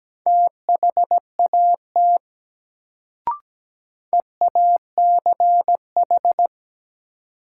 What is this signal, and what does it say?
Morse code, heard as a keyed tone, about 17 wpm, 700 Hz. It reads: THAT EACH
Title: Sets of 2 Words - New Word Drill - 17wpm